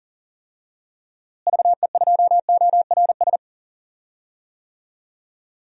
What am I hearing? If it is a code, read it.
Morse code, heard as a keyed tone, about 40 wpm, 700 Hz. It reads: VE2ORS
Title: DXCC-100 Call Signs - 6 Characters - Repeated 3x - 40wpm